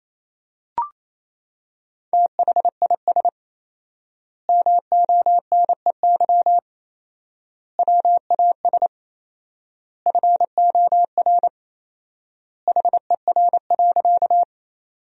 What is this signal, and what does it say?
Morse code, heard as a keyed tone, about 28 wpm, 700 Hz. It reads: THIS MONEY WAS FOR HER.
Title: 4x Word-Spacing - Sentences from Top 400 Words - 28wpm